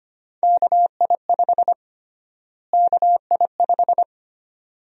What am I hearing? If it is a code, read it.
Morse code, heard as a keyed tone, about 25 wpm, 700 Hz. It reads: KI5 KI5
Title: USA Call Sign Prefixes - Repeated 2x - 25wpm